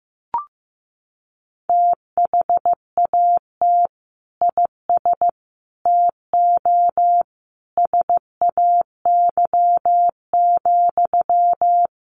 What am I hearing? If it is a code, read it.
Morse code, heard as a keyed tone, about 15 wpm, 700 Hz. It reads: THAT IS TO SAY,
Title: Sentences from Top 700 Words 15wpm